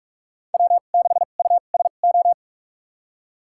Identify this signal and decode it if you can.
Morse code, heard as a keyed tone, about 43 wpm, spaced as for 32 wpm, 700 Hz. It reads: W6USO
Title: USA Call Signs - 5 Characters - 2x Character Spacing - 45wpm